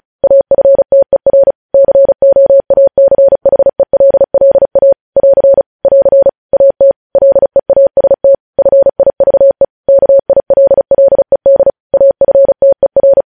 Morse code, heard as a keyed tone, about 36 wpm, 550 Hz. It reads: AFTER COACHELLA <AR> <AR> AT LEAST FIVE KILLED AFTER
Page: news headlines in morse code at 35 wpm